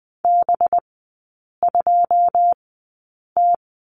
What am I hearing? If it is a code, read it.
Morse code, heard as a keyed tone, about 19 wpm, spaced as for 8 wpm, 700 Hz. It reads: B2T
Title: 184 Lesson 30e - Letter b - Three Characters 20wpm